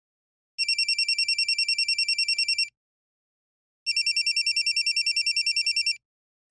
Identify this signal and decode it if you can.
There is no code.